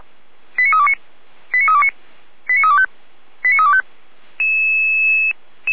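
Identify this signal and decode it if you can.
Five-tone selective call (ZVEI), heard as quick runs of five tones, tones of about 80 ms. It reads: 89329 89329 89325 89325
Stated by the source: ZVEI five-tone call received by SDR.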